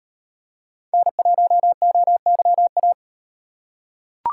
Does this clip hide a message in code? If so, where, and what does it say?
Yes: Morse code from 0.9 to 2.9 s, heard as a keyed tone, about 36 wpm, 700 Hz. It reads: N1OYA